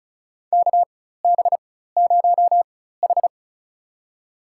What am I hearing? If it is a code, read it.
Morse code, heard as a keyed tone, about 36 wpm, spaced as for 16 wpm, 700 Hz. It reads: KB0H
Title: USA Call Signs - 4 Characters - 4x Character Spacing - 35wpm